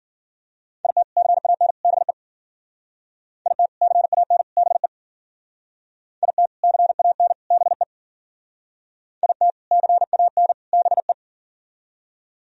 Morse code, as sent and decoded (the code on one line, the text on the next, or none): .. - / -.-. .- -. / -... / .. - / -.-. .- -. / -... / .. - / -.-. .- -. / -... / .. - / -.-. .- -. / -... .
IT CAN BE IT CAN BE IT CAN BE IT CAN BE